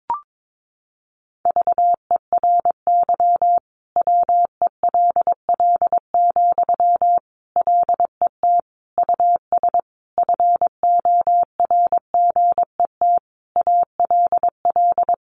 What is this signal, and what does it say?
Morse code, heard as a keyed tone, about 22 wpm, 700 Hz. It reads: VERY WELL, LET US FORGET ALL
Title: Sentences from Top 1200 Words 22wpm